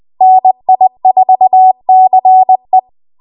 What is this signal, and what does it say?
Morse code, heard as a keyed tone, about 20 wpm, 750 Hz. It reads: NI4CE